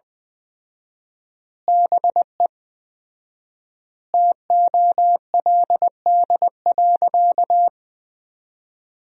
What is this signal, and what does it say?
Morse code, heard as a keyed tone, about 20 wpm, 700 Hz. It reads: BE TOLD.
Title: 4x Word-Spacing - Sentences from Top 400 Words - 20wpm